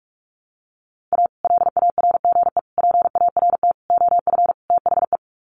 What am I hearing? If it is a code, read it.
Morse code, heard as a keyed tone, about 44 wpm, 700 Hz. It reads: A LARGE PART OF THE
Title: Sets of 5 Words Encore2 45wpm